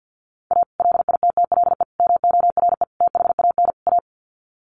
Morse code, heard as a keyed tone, about 48 wpm, 700 Hz. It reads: A LITTLE MORE THAN A
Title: Sets of 5 Words Encore2 50wpm